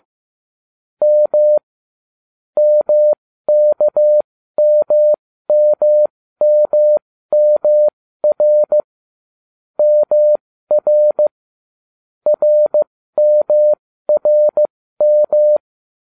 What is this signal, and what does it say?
Morse code, heard as a keyed tone, about 15 wpm, spaced as for 13 wpm, 600 Hz. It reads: M MKMMMMR MR RMRM